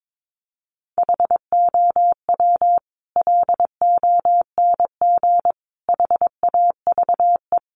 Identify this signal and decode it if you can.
Morse code, heard as a keyed tone, about 22 wpm, 700 Hz. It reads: HOW LONG HAVE
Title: Sentences from Top 200 Words 22wpm